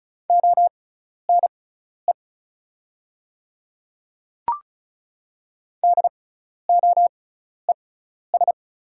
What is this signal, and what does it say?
Morse code, heard as a keyed tone, about 36 wpm, spaced as for 12 wpm, 690 Hz. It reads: ONE DOES
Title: 6x Character Spacing - Top 300 Words - 35wpm